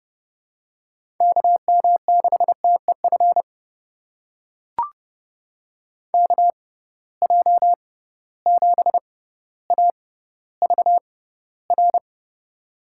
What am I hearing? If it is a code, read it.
Morse code, heard as a keyed tone, about 32 wpm, spaced as for 16 wpm, 700 Hz. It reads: KM6TEF KJ7AVR
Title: USA Call Signs - 6 Characters - 6x Character Spacing - 30wpm